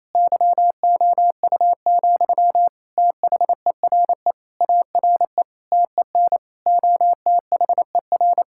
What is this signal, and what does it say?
Morse code, heard as a keyed tone, about 28 wpm, 700 Hz. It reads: YOU, THERE ARE TEN OTHER
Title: Sentences from Top 1000 Words - Review Part 6 - 28wpm